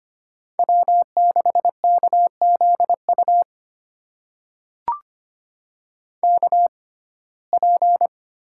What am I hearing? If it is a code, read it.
Morse code, heard as a keyed tone, about 25 wpm, spaced as for 18 wpm, 700 Hz. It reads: W6KZU KP
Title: USA Call Signs - 5 Characters - 6x Character Spacing - 25wpm